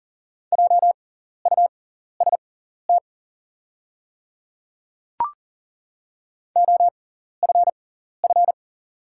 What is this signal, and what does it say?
Morse code, heard as a keyed tone, about 40 wpm, spaced as for 14 wpm, 700 Hz. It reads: JUST OFF
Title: 6x Character Spacing - Top 500 Words - 40wpm